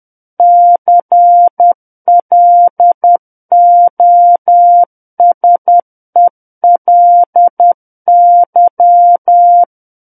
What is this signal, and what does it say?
Morse code, heard as a keyed tone, about 10 wpm, 700 Hz. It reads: CLOSELY
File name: Book One / 01-Chapter 1 - 10 WPM